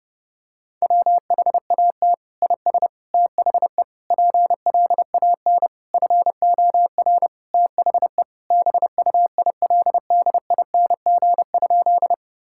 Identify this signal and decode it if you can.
Morse code, heard as a keyed tone, about 30 wpm, 700 Hz. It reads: WHAT IS THE PLAN FOR THE BUILDING?
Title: Sentences from Top 1000 Words - Review Part 5 - 30wpm